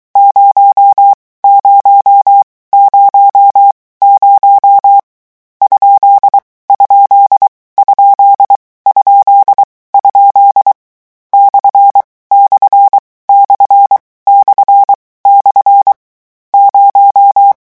Morse code, heard as a keyed tone, about 23 wpm, 800 Hz. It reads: 0 0 0 0 ? ? ? ? ? / / / / / 0
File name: morse mp3 / lecon13-vitesse 22